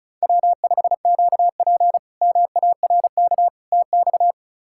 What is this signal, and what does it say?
Morse code, heard as a keyed tone, about 34 wpm, 700 Hz. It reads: W5QP MARK TX
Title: K1USN SST QSO Practice - 35wpm